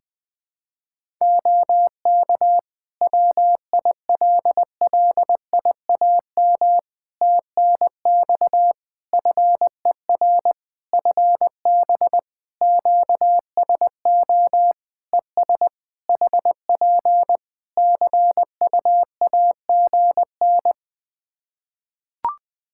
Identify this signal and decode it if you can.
Morse code, heard as a keyed tone, about 20 wpm, 700 Hz. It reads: OK WILLIAM TNX FER FB QSO ES HP CUAGN
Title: Practice QSO - Exchange 4 - Ending - 20wpm